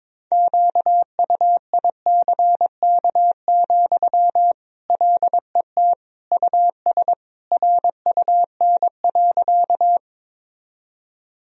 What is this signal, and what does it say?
Morse code, heard as a keyed tone, about 22 wpm, 700 Hz. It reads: QUICK, LET US RUN.
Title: Sentences from Top 1000 Words - Review Part 1 - 22wpm